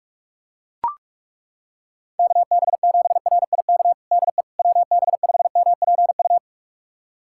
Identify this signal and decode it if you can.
Morse code, heard as a keyed tone, about 42 wpm, 690 Hz. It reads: KB7RIK DE WB5MPU
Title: USA Callsign DE Callsign - 6 Characters - 45wpm